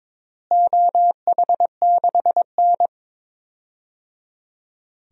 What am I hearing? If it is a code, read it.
Morse code, heard as a keyed tone, about 22 wpm, 700 Hz. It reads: OH6N
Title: DXCC-100 Call Signs - 4 Characters - Repeated 3x - 22wpm